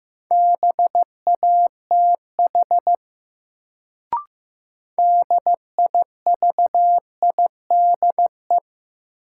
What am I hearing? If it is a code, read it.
Morse code, heard as a keyed tone, about 15 wpm, 700 Hz. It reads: BATH DIVIDE
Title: Sentences from Top 1600 Words - New Word Drill - 15wpm